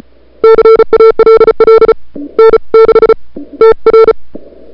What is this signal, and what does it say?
Morse code, heard as a keyed tone, about 35 wpm, spaced as for 27 wpm, 440 Hz. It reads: CALL N6TR
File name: n6tr / Fab2007CWSprintHour1